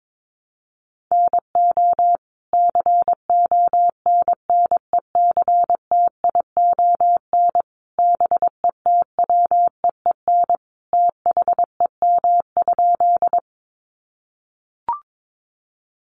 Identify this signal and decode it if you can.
Morse code, heard as a keyed tone, about 22 wpm, 700 Hz. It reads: NO CONNECTION BETWEEN THEM?
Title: Sentences from Top 1300 Words 22wpm